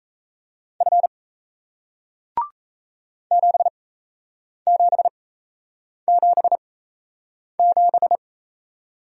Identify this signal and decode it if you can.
Morse code, heard as a keyed tone, about 38 wpm, spaced as for 8 wpm, 700 Hz. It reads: F 7777